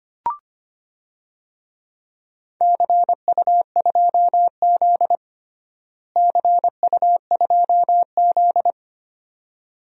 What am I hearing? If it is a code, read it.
Morse code, heard as a keyed tone, about 25 wpm, 700 Hz. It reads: CU2Z CU2Z
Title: DXCC-100 Call Signs - 4 Characters - Repeated 2x - 25wpm